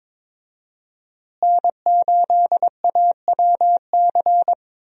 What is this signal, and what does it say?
Morse code, heard as a keyed tone, about 22 wpm, 700 Hz. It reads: N8AWC
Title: USA Call Signs - 5 Characters - Repeated 2x - 22wpm